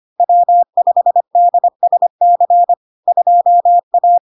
Morse code, heard as a keyed tone, about 25 wpm, 700 Hz. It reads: W5DSC 2A
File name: LARK-FD-2022-Log-in-CW-25wpm-700-Hz